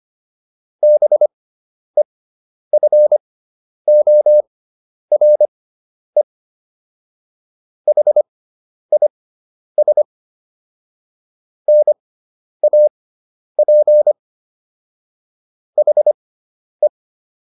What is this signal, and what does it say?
Morse code, heard as a keyed tone, about 25 wpm, spaced as for 10 wpm, 600 Hz. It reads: BEFORE HIS NAP HE